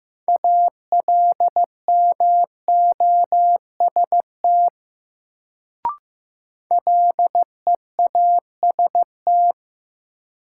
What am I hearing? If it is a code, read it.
Morse code, heard as a keyed tone, about 15 wpm, 700 Hz. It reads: ALMOST LEAST
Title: Sentences from Top 700 Words - New Word Drill - 15wpm